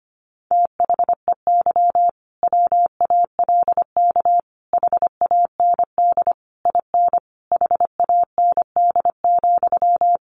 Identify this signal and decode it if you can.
Morse code, heard as a keyed tone, about 25 wpm, 700 Hz. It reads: THEY WALK HAND IN HAND,
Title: Sentences from Top 1400 Words 25wpm